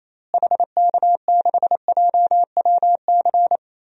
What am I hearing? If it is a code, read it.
Morse code, heard as a keyed tone, about 28 wpm, 700 Hz. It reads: HK6JWC
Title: DXCC-100 Call Signs - 6 Characters - Repeated 3x - 28wpm